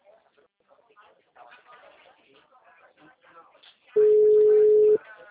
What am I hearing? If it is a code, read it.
Ringback tone (Europe, 425 Hz), with speech.
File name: out- -2009- - - .38434